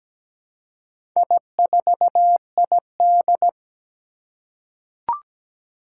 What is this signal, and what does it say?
Morse code, heard as a keyed tone, about 17 wpm, 700 Hz. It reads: I4ID
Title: Four Challenging Letter-Number Combinations 17wpm